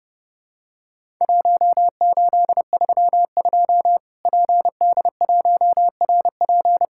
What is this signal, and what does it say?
Morse code, heard as a keyed tone, about 30 wpm, 700 Hz. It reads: 1832 PD1RP
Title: Worked All Europe DX Contest - QTC Exchange - 30wpm